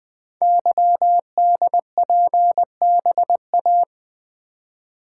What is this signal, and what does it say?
Morse code, heard as a keyed tone, about 20 wpm, 700 Hz. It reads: YDPBA
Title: Code Groups - 5 Letters - 20wpm